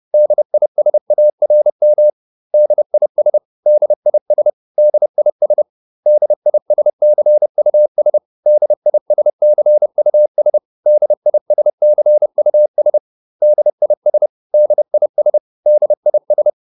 Morse code, heard as a keyed tone, about 30 wpm, 600 Hz. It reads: DISARM DIS DIS DIS DISCUS DISCUS DISCUS DIS DIS DIS